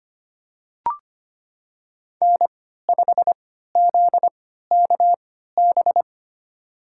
Morse code, heard as a keyed tone, about 25 wpm, spaced as for 14 wpm, 700 Hz. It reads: N5ZKB
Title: USA Call Signs - 5 Characters - 3x Character Spacing - 25wpm